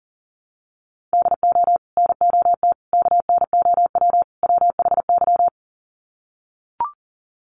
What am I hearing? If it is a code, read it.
Morse code, heard as a keyed tone, about 40 wpm, 700 Hz. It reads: DO NOT KNOW WHY
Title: Sets of 4 Words 40wpm